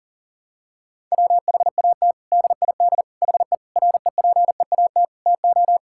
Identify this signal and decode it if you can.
Morse code, heard as a keyed tone, about 37 wpm, 700 Hz. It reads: WHAT DID HE REPEAT TO
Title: Sentences from Top 1600 Words 40wpm